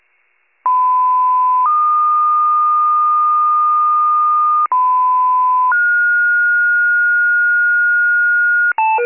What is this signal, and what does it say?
Two-tone page, heard as a one-second tone, then a three-second tone, about 1010 then 1220 Hz, about 1010 then 1430 Hz, with noise. It is heard from 0.6 to 8.7 s.